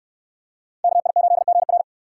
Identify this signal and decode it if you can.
Morse code, heard as a keyed tone, about 69 wpm, 690 Hz. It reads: KI8GR